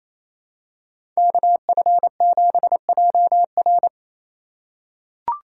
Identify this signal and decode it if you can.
Morse code, heard as a keyed tone, about 28 wpm, 700 Hz. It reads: KF7JR
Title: USA Call Signs - 5 Characters - 2x Character Spacing - 28wpm